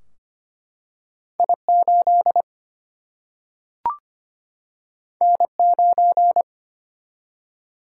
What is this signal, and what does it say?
Morse code, heard as a keyed tone, about 25 wpm, 700 Hz. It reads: I 8 N 9
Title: Single-Letter Single-Number 25wpm